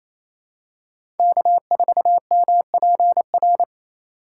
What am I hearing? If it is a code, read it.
Morse code, heard as a keyed tone, about 28 wpm, 700 Hz. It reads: K4MPR
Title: USA Call Signs - 6x Character Spacing - 28wpm